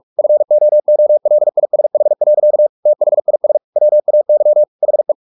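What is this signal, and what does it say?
Morse code, heard as a keyed tone, about 44 wpm, 600 Hz. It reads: FOOLISH. THIS WAY HE
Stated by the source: Morse practice at 45 wpm, the novel Never by Ken Follett in English, from CW med Gnister.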